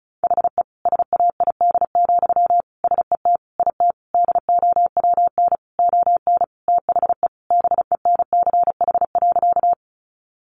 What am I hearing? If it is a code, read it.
Morse code, heard as a keyed tone, about 34 wpm, 700 Hz. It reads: HE SAID, SET IT DOWN ON THE BENCH.
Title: Sentences from Top 1400 Words 35wpm